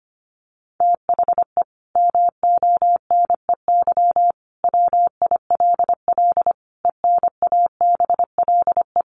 Morse code, heard as a keyed tone, about 25 wpm, 700 Hz. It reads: THE MONEY WILL ENABLE
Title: Sentences from Top 1500 Words 25wpm